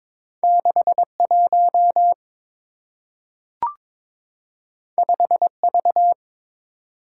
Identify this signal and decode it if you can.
Morse code, heard as a keyed tone, about 22 wpm, 690 Hz. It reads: 61 5V
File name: practice / 22wpm / Two Letter-Number Combinations - 22wpm